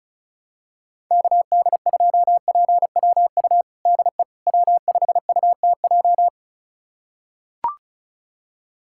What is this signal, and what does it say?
Morse code, heard as a keyed tone, about 34 wpm, 700 Hz. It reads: KD2PWU DE W5UTJ
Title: USA Callsign DE Callsign 35wpm